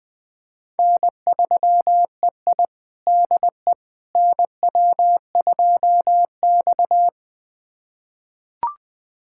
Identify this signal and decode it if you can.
Morse code, heard as a keyed tone, about 20 wpm, 700 Hz. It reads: N3EI DE NW2X
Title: USA Callsign DE Callsign - 4 Characters - 20wpm